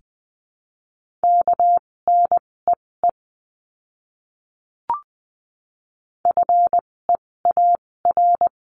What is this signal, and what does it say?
Morse code, heard as a keyed tone, about 19 wpm, spaced as for 16 wpm, 700 Hz. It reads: KNEE FEAR
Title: 256 Lesson 42a - Words 12-20wpm